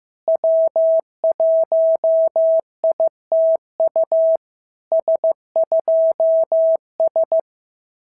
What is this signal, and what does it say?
Morse code, heard as a keyed tone, about 16 wpm, 650 Hz. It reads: W1ITU S2S
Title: SOTA QSO Practice 15wpm